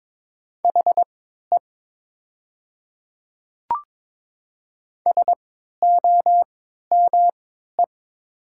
Morse code, heard as a keyed tone, about 22 wpm, spaced as for 12 wpm, 700 Hz. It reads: HE SOME
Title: 3x Character Spacing - Top 100 Words - 22wpm